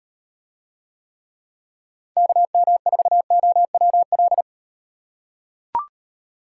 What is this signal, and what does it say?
Morse code, heard as a keyed tone, about 38 wpm, 700 Hz. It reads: KM4OWL